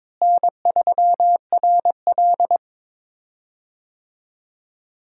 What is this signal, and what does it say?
Morse code, heard as a keyed tone, about 22 wpm, 700 Hz. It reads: N3RL
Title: USA Call Signs - 4 Characters - Repeated 3x - 22wpm